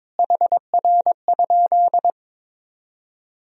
Morse code, heard as a keyed tone, about 22 wpm, 700 Hz. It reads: HR?
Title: Sentences from Top 200 Words and Abbreviations 22wpm